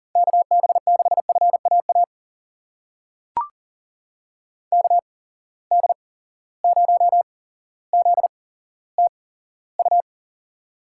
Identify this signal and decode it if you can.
Morse code, heard as a keyed tone, about 39 wpm, spaced as for 17 wpm, 700 Hz. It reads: KB6FAA KD0ZTU
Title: USA Call Signs - 6 Characters - 8x Character Spacing - 40wpm